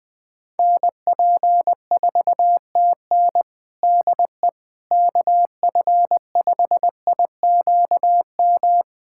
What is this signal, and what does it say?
Morse code, heard as a keyed tone, about 20 wpm, 700 Hz. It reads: NP4TN DE KF5IQM
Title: USA Callsign DE Callsign 20wpm